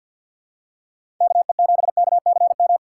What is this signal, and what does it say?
Morse code, heard as a keyed tone, about 47 wpm, 700 Hz. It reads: KE7KCM
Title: USA Call Signs - 3x Character Spacing - 50wpm